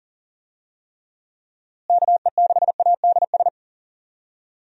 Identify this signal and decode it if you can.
Morse code, heard as a keyed tone, about 40 wpm, 700 Hz. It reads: KE6ADS
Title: USA Call Signs - 6 Characters - Repeated 2x - 40wpm